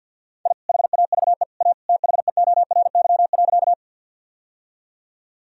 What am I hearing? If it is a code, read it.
Morse code, heard as a keyed tone, about 52 wpm, 700 Hz. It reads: I HAVE A THEORY.
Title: Sentences from Top 1000 Words - Review Part 3 - 50wpm